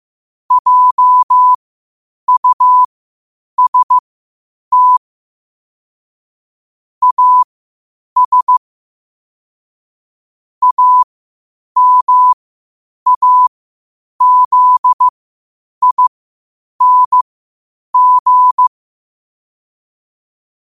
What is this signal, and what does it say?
Morse code, heard as a keyed tone, about 15 wpm, spaced as for 8 wpm, 1000 Hz. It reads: JUST AS AMAZING
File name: qotdfiles / 8wpm